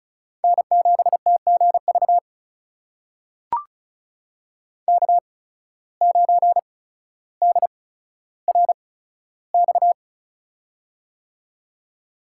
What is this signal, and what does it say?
Morse code, heard as a keyed tone, about 36 wpm, spaced as for 15 wpm, 700 Hz. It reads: N7TGV K9DRX
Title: USA Call Signs - 5 Characters - 8x Character Spacing - 35wpm